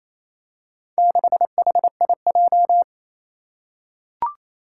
Morse code, heard as a keyed tone, about 27 wpm, 700 Hz. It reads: 6HIJ
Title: Four Challenging Letter-Number Combinations 28wpm